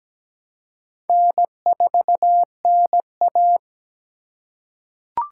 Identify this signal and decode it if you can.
Morse code, heard as a keyed tone, about 17 wpm, 700 Hz. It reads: N4NA
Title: USA Call Signs - 3x Character Spacing - 17wpm